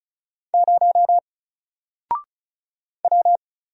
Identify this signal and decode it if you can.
Morse code, heard as a keyed tone, about 36 wpm, 700 Hz. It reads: 0 W
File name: Single-Letter-Number-35wpm